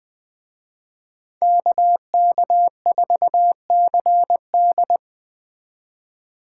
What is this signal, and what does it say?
Morse code, heard as a keyed tone, about 20 wpm, 700 Hz. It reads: KK4CD